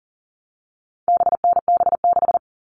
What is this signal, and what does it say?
Morse code, heard as a keyed tone, about 40 wpm, 700 Hz. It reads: BNB6
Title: All Cut Numbers - 4-Digit 40wpm